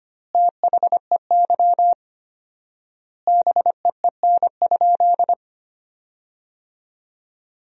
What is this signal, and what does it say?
Morse code, heard as a keyed tone, about 25 wpm, 700 Hz. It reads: THEY BEEN?
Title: 4x Word-Spacing - Sentences from Top 200 Words - 25wpm